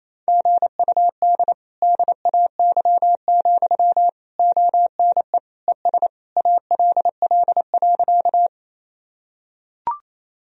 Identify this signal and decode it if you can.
Morse code, heard as a keyed tone, about 28 wpm, 700 Hz. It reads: GUD DAY, ONE ES ALL.
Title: Sentences from Top 200 Words and Abbreviations 28wpm